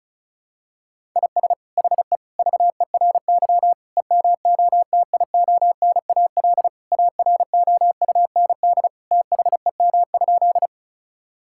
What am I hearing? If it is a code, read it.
Morse code, heard as a keyed tone, about 34 wpm, 700 Hz. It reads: IS HE VERY EMOTIONAL AROUND THEM?
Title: Sentences from Top 1000 Words - Review Part 5 - 35wpm